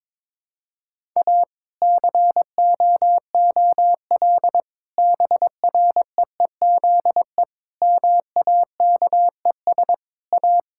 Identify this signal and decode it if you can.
Morse code, heard as a keyed tone, about 22 wpm, 700 Hz. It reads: A COOL BREEZE MAKES A
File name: Difficult Sentences with Hints - 22wpm